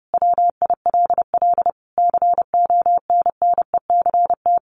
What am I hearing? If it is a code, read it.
Morse code, heard as a keyed tone, about 30 wpm, 700 Hz. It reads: WILL CONNECT
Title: Sentences from Top 1500 Words 30wpm